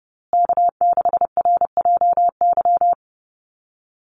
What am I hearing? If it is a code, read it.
Morse code, heard as a keyed tone, about 30 wpm, 700 Hz. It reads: K6RJY
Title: USA Call Signs Repeated 2x 30wpm